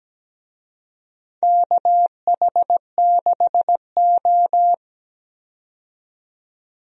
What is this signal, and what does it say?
Morse code, heard as a keyed tone, about 17 wpm, 700 Hz. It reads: KH6O
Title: DXCC-100 Call Signs - 4 Characters - 17wpm